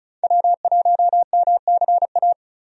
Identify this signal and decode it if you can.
Morse code, heard as a keyed tone, about 36 wpm, 700 Hz. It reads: W1MCA
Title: USA Call Signs - 5 Characters - 35wpm